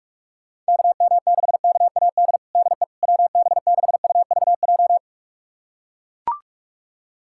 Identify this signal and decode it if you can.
Morse code, heard as a keyed tone, about 41 wpm, 690 Hz. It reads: KM6KAD DE WB6UVJ
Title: USA Callsign DE Callsign - 6 Characters - 45wpm